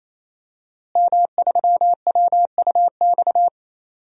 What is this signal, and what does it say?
Morse code, heard as a keyed tone, about 28 wpm, 700 Hz. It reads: M3WUX
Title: DXCC-100 Call Signs - 5 Characters - Repeated 3x - 28wpm